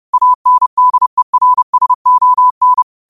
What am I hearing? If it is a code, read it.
Morse code, heard as a keyed tone, about 30 wpm, 1000 Hz. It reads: ANDERSON
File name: qotdfiles / 30wpm